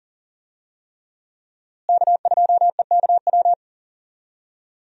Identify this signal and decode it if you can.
Morse code, heard as a keyed tone, about 40 wpm, 700 Hz. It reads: K2EKW
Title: USA Call Signs - 5 Characters - Repeated 3x - 40wpm